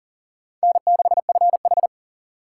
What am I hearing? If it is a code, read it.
Morse code, heard as a keyed tone, about 40 wpm, 700 Hz. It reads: N6FH
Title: USA Call Signs - 4 Characters - 4x Character Spacing - 40wpm